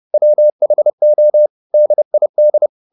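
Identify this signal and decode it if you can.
Morse code, heard as a keyed tone, about 30 wpm, 600 Hz. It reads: WHO DID
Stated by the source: Morse code at 30 wpm, the novel Never by Ken Follett in English, from CW med Gnister.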